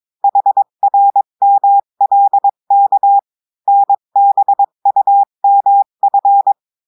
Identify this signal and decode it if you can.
Morse code, heard as a keyed tone, about 22 wpm, 800 Hz. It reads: HRMLK NBUMF
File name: cours-du-mercredi / audios / cours-853-du-2-mai-2024-etxte chiffre et texte clair-a-20-wpm